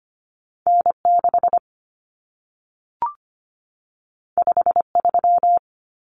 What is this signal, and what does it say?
Morse code, heard as a keyed tone, about 25 wpm, 700 Hz. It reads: N6 53
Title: Common Cut Numbers - 2-Digit 25wpm